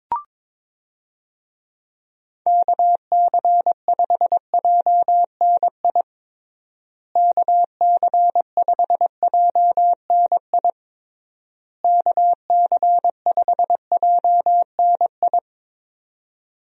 Morse code, heard as a keyed tone, about 22 wpm, 700 Hz. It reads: KC5JNI KC5JNI KC5JNI
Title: USA Call Signs - 6 Characters - Repeated 3x - 22wpm